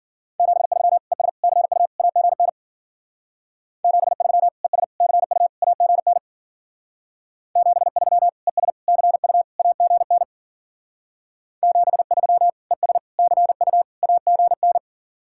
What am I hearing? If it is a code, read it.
Morse code, heard as a keyed tone, about 49 wpm, 700 Hz. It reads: 73 ES CU AGN 73 ES CU AGN 73 ES CU AGN 73 ES CU AGN
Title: Speed-Racing Common QSO Phrases - 40wpm